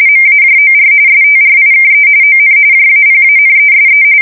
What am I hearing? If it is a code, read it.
RTTY, about 45.45 baud. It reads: MAJOR STRIKE ON JAPANESE